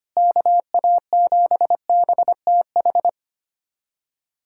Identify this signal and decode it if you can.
Morse code, heard as a keyed tone, about 25 wpm, 700 Hz. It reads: KA7BTH
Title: USA Callsign DE Callsign 25wpm